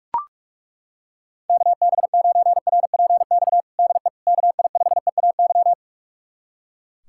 Morse code, heard as a keyed tone, about 42 wpm, 700 Hz. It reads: KB9RPX DE KI5EAY
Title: USA Callsign DE Callsign - 6 Characters - 45wpm